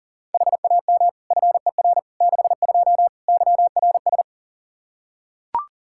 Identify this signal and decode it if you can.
Morse code, heard as a keyed tone, about 37 wpm, 700 Hz. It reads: HAM FER 62 YRS
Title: Practice QSO - Exchange 3 - More Info - 40wpm